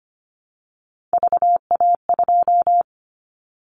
Morse code, heard as a keyed tone, about 25 wpm, 700 Hz. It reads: VA2
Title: Callsign Prefixes for Top 50 DXCC - 25wpm